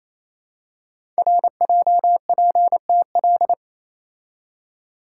Morse code, heard as a keyed tone, about 27 wpm, 700 Hz. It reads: RJPTL
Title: Code Groups - 5 Letters - 28wpm